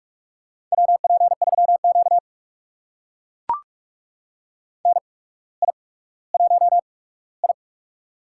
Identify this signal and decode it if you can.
Morse code, heard as a keyed tone, about 42 wpm, spaced as for 19 wpm, 700 Hz. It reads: WP3Q NI1I
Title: USA Call Signs - 4 Characters - 8x Character Spacing - 45wpm